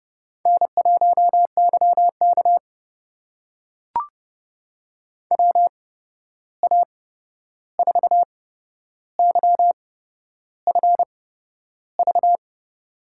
Morse code, heard as a keyed tone, about 30 wpm, spaced as for 11 wpm, 700 Hz. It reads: N1YK WA4YFV
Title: USA Call Signs - 8x Character Spacing - 30wpm